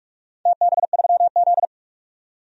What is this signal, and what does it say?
Morse code, heard as a keyed tone, about 41 wpm, 700 Hz. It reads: TB3Z
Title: DXCC-100 Call Signs - 4 Characters - 45wpm